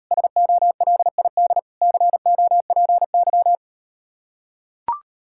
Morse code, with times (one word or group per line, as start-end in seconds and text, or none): 0.1-1.6 SOLID
1.8-3.6 COPY